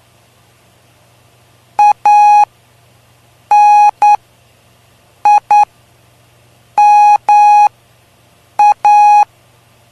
Morse code, heard as a keyed tone, about 9 wpm, spaced as for 6 wpm, 830 Hz. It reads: ANIMA
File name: cw / cwzap / 03-PY2LCD-Li��o 3